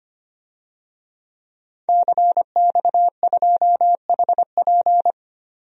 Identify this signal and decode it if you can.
Morse code, heard as a keyed tone, about 25 wpm, 700 Hz. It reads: CX2HP
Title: DXCC-100 Call Signs - 5 Characters - 25wpm